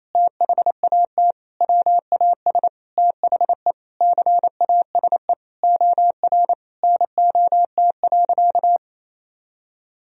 Morse code, heard as a keyed tone, about 28 wpm, 690 Hz. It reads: THAT WAS THE CASE OR NOT.
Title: Sentences from Top 1000 Words - Review Part 3 - 28wpm